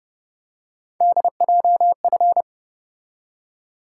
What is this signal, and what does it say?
Morse code, heard as a keyed tone, about 30 wpm, 700 Hz. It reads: DJF